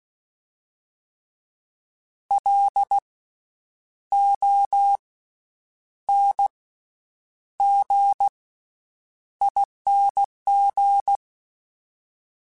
Morse code, heard as a keyed tone, about 16 wpm, 800 Hz. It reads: L O N G ING
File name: Special INGs-mp3s / Special INGs-mp3s / Slow ing @15